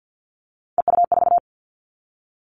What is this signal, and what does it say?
Morse code, heard as a keyed tone, about 50 wpm, 700 Hz. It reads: EU4
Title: Callsign Prefixes for Top 50 DXCC - 50wpm